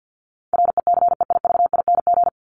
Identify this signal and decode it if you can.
Morse code, heard as a keyed tone, about 50 wpm, 700 Hz. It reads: RECEIVING
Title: ING Suffix - Top 100 Words with Base Words 50wpm